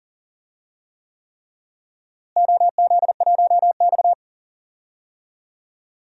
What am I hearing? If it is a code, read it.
Morse code, heard as a keyed tone, about 37 wpm, 700 Hz. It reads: OZ1X
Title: DXCC-100 Call Signs - 4 Characters - Repeated 3x - 40wpm